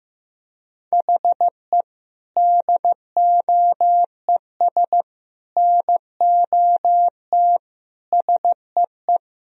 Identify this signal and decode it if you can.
Morse code, heard as a keyed tone, about 15 wpm, 700 Hz. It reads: HE DOES NOT SEE
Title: Sentences from Top 1000 Words - Review Part 1 - 15wpm